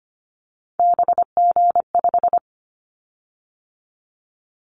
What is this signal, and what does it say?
Morse code, heard as a keyed tone, about 25 wpm, 700 Hz. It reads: BG5
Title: Callsign Prefixes for Top 100 DXCC - 25wpm